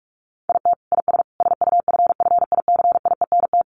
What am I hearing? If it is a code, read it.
Morse code, heard as a keyed tone, about 44 wpm, 700 Hz. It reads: IT IS SUFFICIENT
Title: Sentences from Top 1400 Words 45wpm